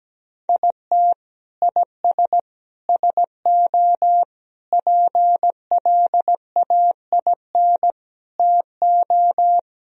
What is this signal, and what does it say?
Morse code, heard as a keyed tone, about 17 wpm, 700 Hz. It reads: IT IS SO PLAIN TO
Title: Sentences from Top 1000 Words - Review Part 2 - 17wpm